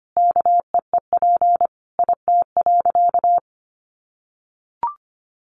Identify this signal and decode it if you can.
Morse code, heard as a keyed tone, about 25 wpm, 700 Hz. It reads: KEEP IT.
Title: Additional Easy Phrases 25wpm